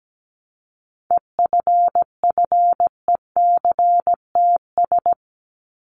Morse code, heard as a keyed tone, about 17 wpm, 700 Hz. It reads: EFFECTS
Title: ES:S Suffix - Top 100 Words with Base Words 17wpm